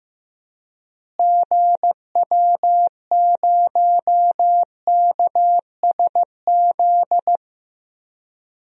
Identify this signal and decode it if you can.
Morse code, heard as a keyed tone, about 15 wpm, 700 Hz. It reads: GW0KSZ
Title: DXCC-100 Call Signs - 6 Characters - Repeated 3x - 15wpm